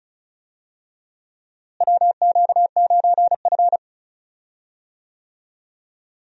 Morse code, heard as a keyed tone, about 36 wpm, 700 Hz. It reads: WQ9F
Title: USA Call Signs - 4 Characters - Repeated 2x - 35wpm